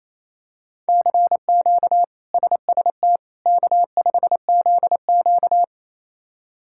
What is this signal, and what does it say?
Morse code, heard as a keyed tone, about 28 wpm, 700 Hz. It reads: CQ SST K5ZQ
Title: K1USN SST QSO Practice - 28wpm